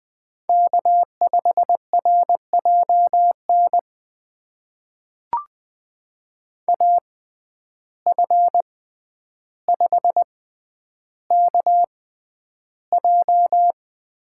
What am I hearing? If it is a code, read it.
Morse code, heard as a keyed tone, about 20 wpm, spaced as for 10 wpm, 700 Hz. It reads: K5RJN AF5KJ
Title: USA Call Signs - 5 Characters - 6x Character Spacing - 20wpm